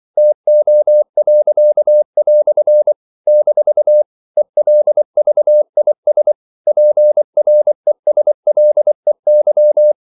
Morse code, heard as a keyed tone, about 24 wpm, 600 Hz. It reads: TO." - ELVIS PRESLEY
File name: CW / Practice / EnglishQuotations0324